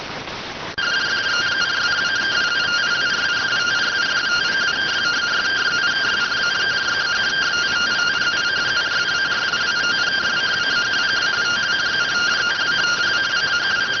Two-tone teleprinter signal, about 170 Hz shift.